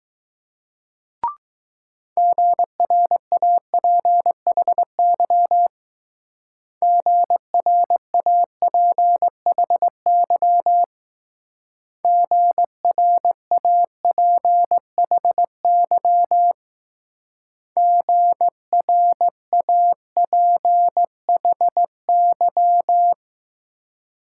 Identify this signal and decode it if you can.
Morse code, heard as a keyed tone, about 19 wpm, 700 Hz. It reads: GRAPHY GRAPHY GRAPHY GRAPHY